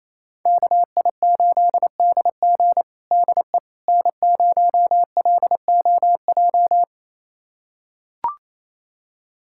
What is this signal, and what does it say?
Morse code, heard as a keyed tone, about 28 wpm, 700 Hz. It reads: KI8DG DE N0LOJ
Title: USA Callsign DE Callsign - 5 Characters - 28wpm